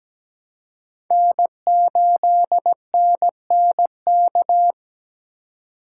Morse code, heard as a keyed tone, about 17 wpm, 700 Hz. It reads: N8NNK
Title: USA Call Signs - 3x Character Spacing - 17wpm